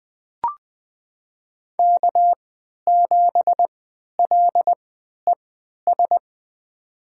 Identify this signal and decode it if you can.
Morse code, heard as a keyed tone, about 21 wpm, spaced as for 11 wpm, 700 Hz. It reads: K7LES
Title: USA Call Signs - 3x Character Spacing - 20wpm